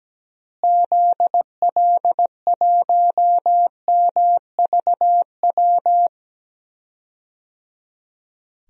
Morse code, heard as a keyed tone, about 17 wpm, 700 Hz. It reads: ZL1MVW
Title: DXCC-100 Call Signs - 6 Characters - Repeated 3x - 17wpm